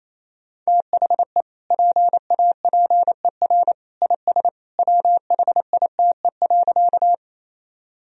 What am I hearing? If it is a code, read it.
Morse code, heard as a keyed tone, about 28 wpm, 700 Hz. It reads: THE PAPER IS WHITE.
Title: Sentences from Top 1000 Words - Review Part 1 - 28wpm